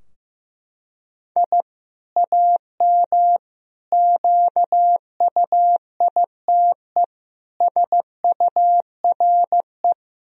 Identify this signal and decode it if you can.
Morse code, heard as a keyed tone, about 15 wpm, 700 Hz. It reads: I AM QUITE SURE
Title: Sentences from Top 1000 Words - Review Part 2 - 15wpm